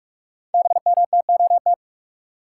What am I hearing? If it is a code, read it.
Morse code, heard as a keyed tone, about 44 wpm, 700 Hz. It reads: BMTOT
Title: Code Groups - 5 Letters - 45wpm